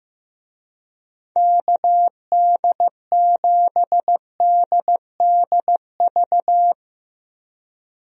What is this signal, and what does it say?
Morse code, heard as a keyed tone, about 15 wpm, 700 Hz. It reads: KD7DDV
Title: USA Call Signs - 6 Characters - Repeated 2x - 15wpm